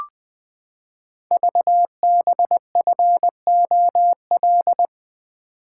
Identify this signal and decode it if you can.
Morse code, heard as a keyed tone, about 20 wpm, 700 Hz. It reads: VBFOL
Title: Code Groups - 5 Letters - 20wpm